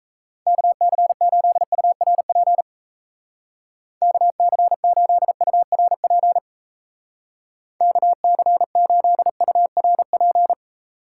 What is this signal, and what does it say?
Morse code, heard as a keyed tone, about 36 wpm, 700 Hz. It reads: KC8URP KC8URP KC8URP